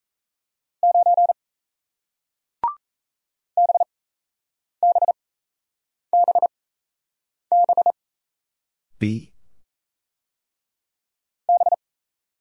Morse code, heard as a keyed tone, about 37 wpm, spaced as for 8 wpm, 700 Hz. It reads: 9 BBBB B